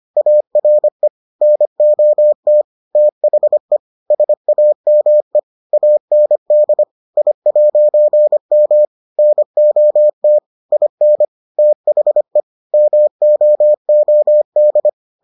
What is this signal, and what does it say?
Morse code, heard as a keyed tone, about 25 wpm, 600 Hz. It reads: ARE NOT THE SAME AND I'M NOT IN THE MOOD